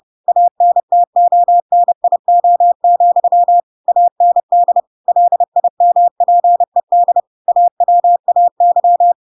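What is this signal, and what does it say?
Morse code, heard as a keyed tone, about 30 wpm, 700 Hz. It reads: ANTONIO, AND LIMPED AWAY
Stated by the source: Chapter 2 of The Adventures of Pinocchio read as a book in Morse code (CW) at 30 wpm, by ik1hge.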